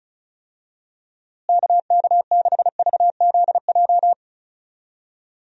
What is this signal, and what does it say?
Morse code, heard as a keyed tone, about 36 wpm, 700 Hz. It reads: KK6VZJ
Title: USA Call Signs - 6 Characters - Repeated 2x - 35wpm